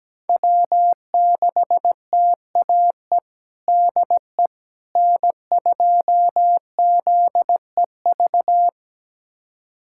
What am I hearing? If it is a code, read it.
Morse code, heard as a keyed tone, about 17 wpm, 700 Hz. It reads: W6TAE DE N2ZEV
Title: USA Callsign DE Callsign - 5 Characters - 17wpm